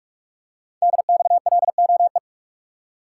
Morse code, heard as a keyed tone, about 44 wpm, 700 Hz. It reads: DXLOE
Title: Code Groups - 5 Letters - 45wpm